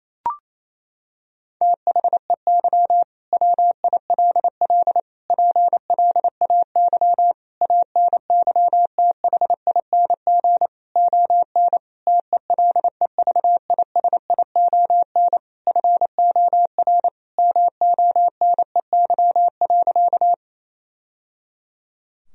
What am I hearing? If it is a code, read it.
Morse code, heard as a keyed tone, about 27 wpm, 700 Hz. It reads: THEY WILL PLAY ANYTHING ON TELEVISION FOR MONEY.
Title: Sentences from Top 1000 Words - Review Part 4 - 28wpm